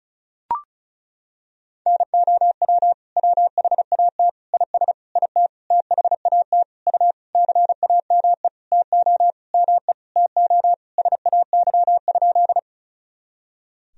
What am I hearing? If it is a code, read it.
Morse code, heard as a keyed tone, about 34 wpm, 690 Hz. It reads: NOW WHAT IS IT THAT U CAME TO ME TO SAY?
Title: Sentences from Top 200 Words and Abbreviations 35wpm